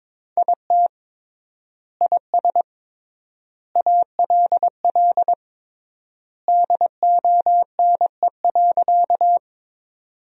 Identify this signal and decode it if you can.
Morse code, heard as a keyed tone, about 22 wpm, 700 Hz. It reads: IT IS ALL DONE.
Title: 3x Word-Spacing - Sentences from Top 500 Words - 22wpm